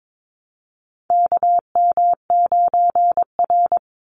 Morse code, heard as a keyed tone, about 22 wpm, 700 Hz. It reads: KM9R
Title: Call Signs - 22wpm